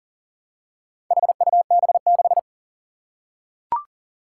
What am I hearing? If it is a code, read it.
Morse code, heard as a keyed tone, about 40 wpm, 700 Hz. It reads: HUB6
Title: Four Challenging Letter-Number Combinations 40wpm